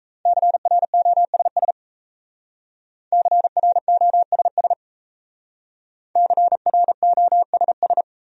Morse code, heard as a keyed tone, about 36 wpm, 700 Hz. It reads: CROSS CROSS CROSS